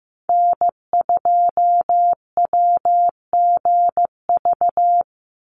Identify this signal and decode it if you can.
Morse code, heard as a keyed tone, about 15 wpm, 700 Hz. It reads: N2WGV
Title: USA Call Signs 15wpm